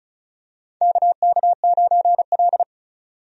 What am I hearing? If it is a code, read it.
Morse code, heard as a keyed tone, about 36 wpm, 700 Hz. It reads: KK9L
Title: USA Call Signs - 4 Characters - 35wpm